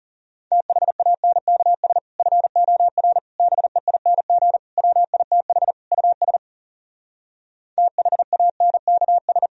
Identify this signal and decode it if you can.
Morse code, heard as a keyed tone, about 40 wpm, 700 Hz. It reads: THANKS FOR BEING WITH US THANKS